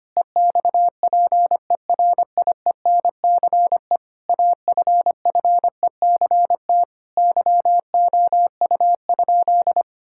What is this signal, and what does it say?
Morse code, heard as a keyed tone, about 25 wpm, 700 Hz. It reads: EXPERIENCE AFFECT YOU?
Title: Sentences from Top 1000 Words - Review Part 6 - 25wpm